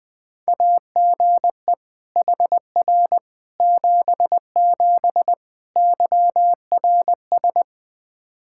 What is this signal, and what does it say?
Morse code, heard as a keyed tone, about 20 wpm, 700 Hz. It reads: AGE HR 77 YRS
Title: Practice QSO - Exchange 3 - More Info - 20wpm